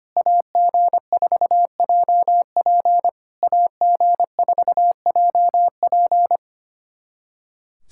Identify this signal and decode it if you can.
Morse code, heard as a keyed tone, about 25 wpm, 700 Hz. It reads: AG4JP AG4JP
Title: SOTA QSO Practice 25wpm